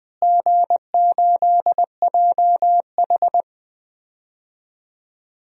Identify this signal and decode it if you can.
Morse code, heard as a keyed tone, about 21 wpm, 700 Hz. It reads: G8JH
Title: DXCC-100 Call Signs - 4 Characters - Repeated 3x - 20wpm